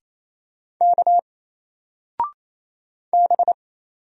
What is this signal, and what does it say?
Morse code, heard as a keyed tone, about 28 wpm, 700 Hz. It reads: K B
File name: Single-Letter-Number-28wpm